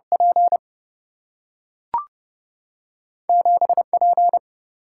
Morse code, heard as a keyed tone, about 30 wpm, 700 Hz. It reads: P 7 P
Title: Single-Letter Single-Number 30wpm